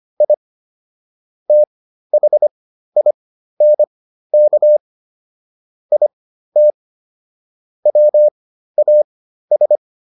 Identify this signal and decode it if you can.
Morse code, heard as a keyed tone, about 25 wpm, spaced as for 13 wpm, 600 Hz. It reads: I THINK IT WAS
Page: /intermediate-practice-files/